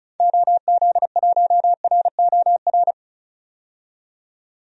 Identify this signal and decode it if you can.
Morse code, heard as a keyed tone, about 34 wpm, 700 Hz. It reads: OZ1ROR
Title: DXCC-100 Call Signs - 6 Characters - 35wpm